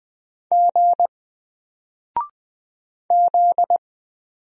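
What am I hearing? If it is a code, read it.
Morse code, heard as a keyed tone, about 20 wpm, 700 Hz. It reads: G Z
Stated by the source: Single letter and number Morse drill at 20 wpm, from N4LCD.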